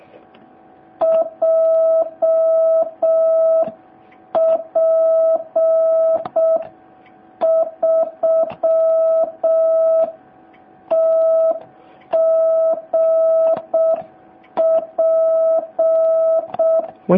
Morse code, heard as a keyed tone, about 6 wpm, 650 Hz. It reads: JP3TGP